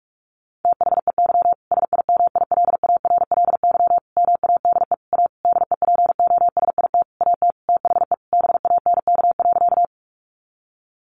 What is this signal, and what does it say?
Morse code, heard as a keyed tone, about 46 wpm, 700 Hz. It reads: THEY SIMILARLY MADE A DEPOSIT AT THE BANK.
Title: Sentences from Top 1400 Words 45wpm